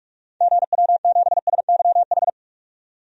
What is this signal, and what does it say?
Morse code, heard as a keyed tone, about 42 wpm, 700 Hz. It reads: GW7SYH
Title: DXCC-100 Call Signs - 6 Characters - 45wpm